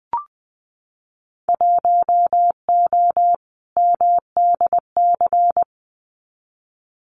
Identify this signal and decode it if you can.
Morse code, heard as a keyed tone, about 20 wpm, 700 Hz. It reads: 1O MDC
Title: Winter Field Day Exchanges 20wpm